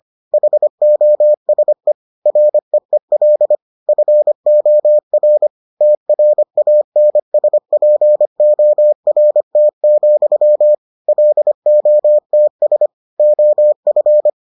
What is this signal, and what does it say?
Morse code, heard as a keyed tone, about 25 wpm, 600 Hz. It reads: HOSE REEL FOR TRANSPORT, LOTS OF